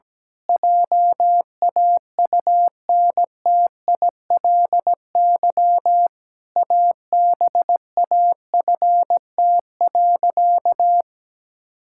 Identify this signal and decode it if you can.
Morse code, heard as a keyed tone, about 17 wpm, 700 Hz. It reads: JAUNTILY ABAFT.
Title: Pangram Challenge v2 - 17wpm